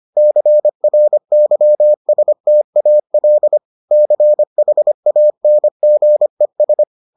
Morse code, heard as a keyed tone, about 25 wpm, 600 Hz. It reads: CRYSTAL CHANGES